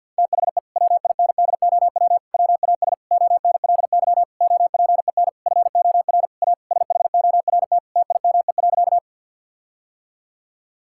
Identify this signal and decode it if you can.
Morse code, heard as a keyed tone, about 52 wpm, 700 Hz. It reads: THE WINDOW WAS ONLY OPEN FOR A SHORT TIME.
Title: Sentences from Top 1600 Words 50wpm